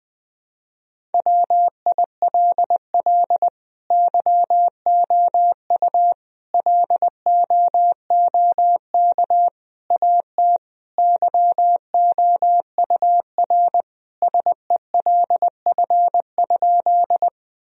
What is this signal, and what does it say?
Morse code, heard as a keyed tone, about 20 wpm, 700 Hz. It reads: WILL YOU LOOK AT YOUR SELF?
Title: Sentences from Top 1000 Words - Review Part 1 - 20wpm